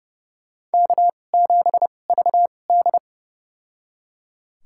Morse code, heard as a keyed tone, about 30 wpm, spaced as for 22 wpm, 700 Hz. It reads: K7VD
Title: USA Call Signs - 4 Characters - 2x Character Spacing - 30wpm